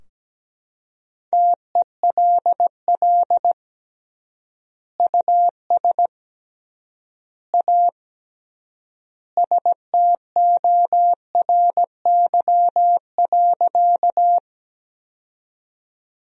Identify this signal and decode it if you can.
Morse code, heard as a keyed tone, about 17 wpm, 700 Hz. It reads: TELL US A STORY.
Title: 3x Word-Spacing - Sentences from Top 300 Words - 17wpm